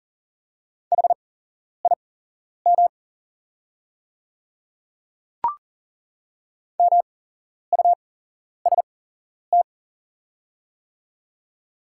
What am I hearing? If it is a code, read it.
Morse code, heard as a keyed tone, about 40 wpm, spaced as for 11 wpm, 700 Hz. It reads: HIM MUST
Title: 8x Character Spacing - Top 200 Words - 40wpm